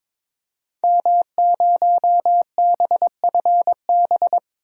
Morse code, heard as a keyed tone, about 21 wpm, 700 Hz. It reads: M0BFB
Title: DXCC-100 Call Signs - 5 Characters - 22wpm